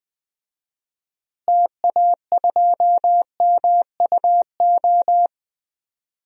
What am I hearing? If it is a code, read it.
Morse code, heard as a keyed tone, about 20 wpm, 700 Hz. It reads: TA2MUO
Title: DXCC-100 Call Signs - 6 Characters - Repeated 3x - 20wpm